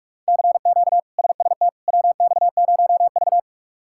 Morse code, heard as a keyed tone, about 46 wpm, 700 Hz. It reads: CQ SST WX0V
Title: K1USN SST QSO Practice - 45wpm